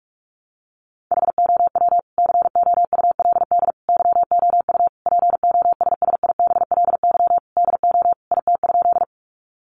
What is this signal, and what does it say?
Morse code, heard as a keyed tone, about 46 wpm, 700 Hz. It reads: HOW COULD YOU POSSIBLY DO IT?
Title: Sentences from Top 1300 Words 45wpm